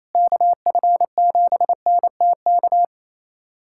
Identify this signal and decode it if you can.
Morse code, heard as a keyed tone, about 28 wpm, 700 Hz. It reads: KF7NTK
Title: USA Callsign DE Callsign - 6 Characters - 28wpm